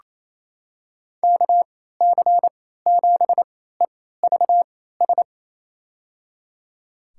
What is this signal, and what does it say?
Morse code, heard as a keyed tone, about 27 wpm, spaced as for 16 wpm, 700 Hz. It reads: KC7EVS
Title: USA Call Signs - 6 Characters - 3x Character Spacing - 28wpm